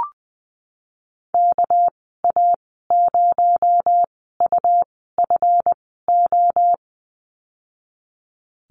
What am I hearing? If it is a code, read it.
Morse code, heard as a keyed tone, about 20 wpm, spaced as for 15 wpm, 700 Hz. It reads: KA0UFO
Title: 231 Lesson 37g - Letter x - Call Signs 20wpm